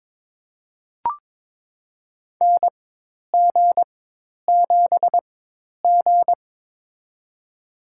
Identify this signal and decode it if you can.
Morse code, heard as a keyed tone, about 22 wpm, spaced as for 10 wpm, 700 Hz. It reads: NG7G
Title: USA Call Signs - 4x Character Spacing - 22wpm